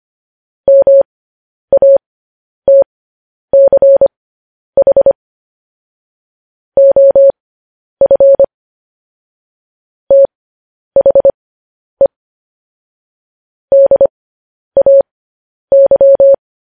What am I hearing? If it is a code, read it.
Morse code, heard as a keyed tone, about 25 wpm, spaced as for 10 wpm, 550 Hz. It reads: MATCH OF THE DAY
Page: News Headlines in Morse Code at 10 WPM